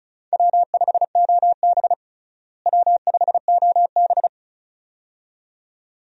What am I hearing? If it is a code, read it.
Morse code, heard as a keyed tone, about 35 wpm, 700 Hz. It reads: W5OB W5OB
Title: USA Call Signs - 4 Characters - Repeated 2x - 35wpm